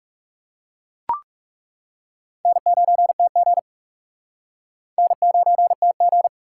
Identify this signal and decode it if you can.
Morse code, heard as a keyed tone, about 37 wpm, 700 Hz. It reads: N9TG N9TG
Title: Speed Racing - USA 1x2 Call Signs 30wpm